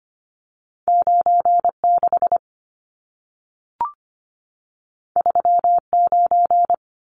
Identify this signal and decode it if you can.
Morse code, heard as a keyed tone, about 25 wpm, 700 Hz. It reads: 96 39
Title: Numbers - 2-Digit 25wpm